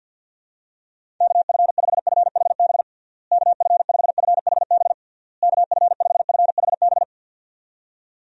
Morse code, heard as a keyed tone, about 51 wpm, 700 Hz. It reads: KF5FHB KF5FHB KF5FHB
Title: USA Call Signs - 6 Characters - Repeated 3x - 50wpm